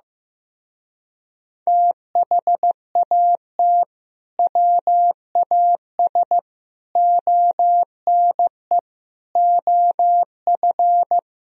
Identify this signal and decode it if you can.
Morse code, heard as a keyed tone, about 15 wpm, 700 Hz. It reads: THAT WAS ONE OF
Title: Speed-Racing - Sets of 5 Words 15wpm